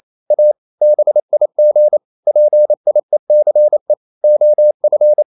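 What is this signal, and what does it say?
Morse code, heard as a keyed tone, about 28 wpm, 600 Hz. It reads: A BIG PIECE OF
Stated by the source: Morse code practice at 28 wpm, text of Ken Follett's novel Never in English, from CW med Gnister.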